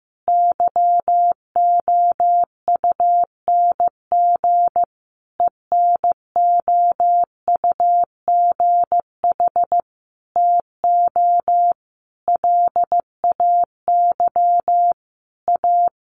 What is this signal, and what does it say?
Morse code, heard as a keyed tone, about 15 wpm, 700 Hz. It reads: YOUNG ENOUGH TO LAY A
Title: Sentences from Top 1200 Words 15wpm